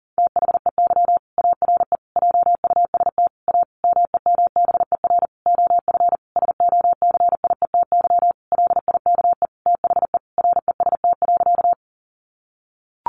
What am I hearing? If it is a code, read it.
Morse code, heard as a keyed tone, about 37 wpm, 700 Hz. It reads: THEY ARE JUST A MEMBER OF SOCIETY LIKE THE REST.
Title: Sentences from Top 1100 Words 40wpm